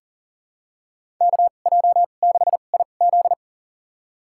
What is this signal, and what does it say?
Morse code, heard as a keyed tone, about 40 wpm, spaced as for 29 wpm, 700 Hz. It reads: KJ6IZ
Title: USA Call Signs - 5 Characters - 2x Character Spacing - 40wpm